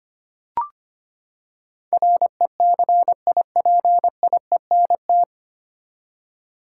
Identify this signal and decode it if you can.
Morse code, heard as a keyed tone, about 25 wpm, 700 Hz. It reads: RECIPIENT
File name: Nine Letter Words - 25wpm